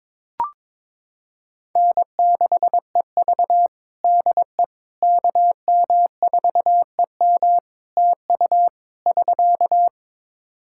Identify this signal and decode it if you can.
Morse code, heard as a keyed tone, about 21 wpm, 700 Hz. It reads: N6EV DE KM4EM TU <SK>
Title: Practice QSO - Exchange 4 - Ending - 22wpm